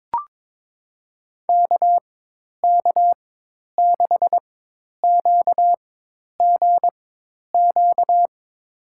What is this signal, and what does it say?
Morse code, heard as a keyed tone, about 22 wpm, spaced as for 10 wpm, 700 Hz. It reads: KK6QGQ
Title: USA Call Signs - 6 Characters - 4x Character Spacing - 22wpm